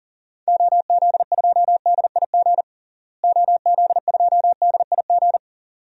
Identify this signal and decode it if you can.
Morse code, heard as a keyed tone, about 40 wpm, 700 Hz. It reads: OZ2DIG OZ2DIG
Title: DXCC-100 Call Signs - 6 Characters - Repeated 2x - 40wpm